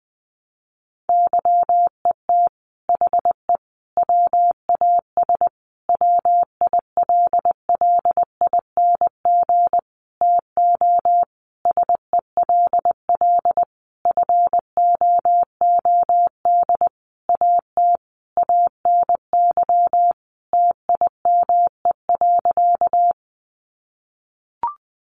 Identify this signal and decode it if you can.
Morse code, heard as a keyed tone, about 19 wpm, 700 Hz. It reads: YET HE WAS WILLING TO SELL FOOD AT ANY TIME.
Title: Sentences from Top 1100 Words 20wpm